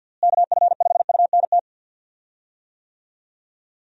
Morse code, heard as a keyed tone, about 48 wpm, 700 Hz. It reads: KF5UNT
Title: USA Call Signs - 6 Characters - Repeated 3x - 50wpm